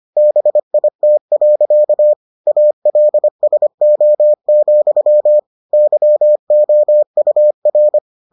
Morse code, heard as a keyed tone, about 25 wpm, 600 Hz. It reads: BIT. ALSO, YOUR